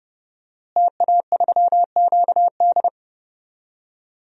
Morse code, heard as a keyed tone, about 30 wpm, 700 Hz. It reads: TA3QD
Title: DXCC-100 Call Signs - 5 Characters - Repeated 3x - 30wpm